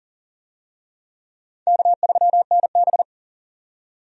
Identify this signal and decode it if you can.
Morse code, heard as a keyed tone, about 40 wpm, 700 Hz. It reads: K3NB
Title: USA Call Signs - 4 Characters - 40wpm